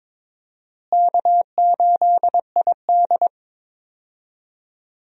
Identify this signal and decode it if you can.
Morse code, heard as a keyed tone, about 21 wpm, 700 Hz. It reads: K8ID
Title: USA Call Signs - 4 Characters - Repeated 2x - 22wpm